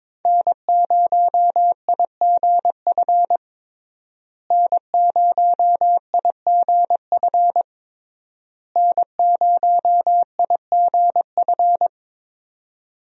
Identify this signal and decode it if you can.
Morse code, heard as a keyed tone, about 22 wpm, 700 Hz. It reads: N0IGF N0IGF N0IGF
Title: USA Call Signs - 5 Characters - Repeated 3x - 22wpm